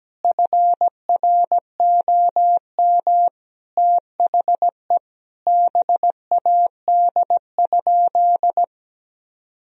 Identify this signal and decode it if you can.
Morse code, heard as a keyed tone, about 17 wpm, 700 Hz. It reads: FROM THE BAD?
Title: Sentences from Top 1000 Words - Review Part 1 - 17wpm